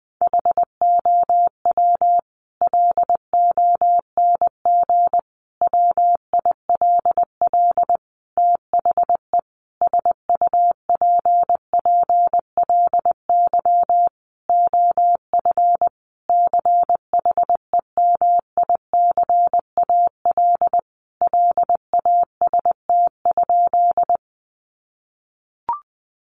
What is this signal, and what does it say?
Morse code, heard as a keyed tone, about 20 wpm, 700 Hz. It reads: HOW LONG WILL THE SUPPLY OF CHEMICAL LAST?
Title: Sentences from Top 1200 Words 20wpm